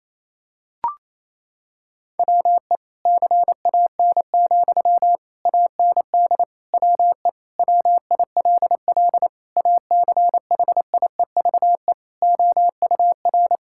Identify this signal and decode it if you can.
Morse code, heard as a keyed tone, about 28 wpm, 700 Hz. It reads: WE CAN, AND WE WILL ACHIEVE OUR
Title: Sentences from Top 1000 Words - Review Part 6 - 28wpm